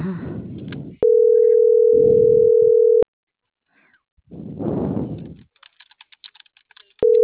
Ringback tone (North America), 440 + 480 Hz.